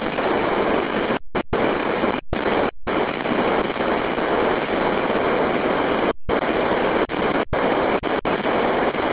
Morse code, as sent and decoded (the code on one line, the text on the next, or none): none